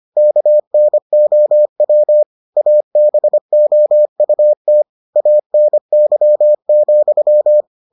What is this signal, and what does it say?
Morse code, heard as a keyed tone, about 25 wpm, 600 Hz. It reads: KNOW ABOUT ANY,